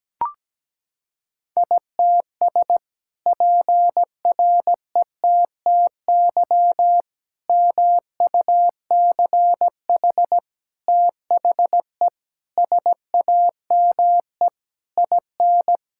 Morse code, heard as a keyed tone, about 17 wpm, 700 Hz. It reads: ITS PRETTY MUCH THE SAME IN
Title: Sentences from Top 1000 Words - Review Part 6 - 17wpm